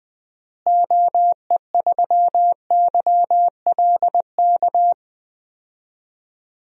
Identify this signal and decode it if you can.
Morse code, heard as a keyed tone, about 21 wpm, 700 Hz. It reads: OE3YLK
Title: DXCC-100 Call Signs - 6 Characters - Repeated 3x - 20wpm